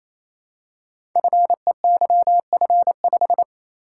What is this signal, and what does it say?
Morse code, heard as a keyed tone, about 27 wpm, 700 Hz. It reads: FEYF5
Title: Code Groups - 5 Letter:Numbers - 28wpm